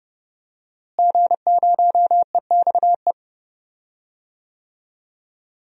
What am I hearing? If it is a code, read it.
Morse code, heard as a keyed tone, about 30 wpm, 700 Hz. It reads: G0EXE
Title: DXCC-100 Call Signs - 5 Characters - Repeated 2x - 30wpm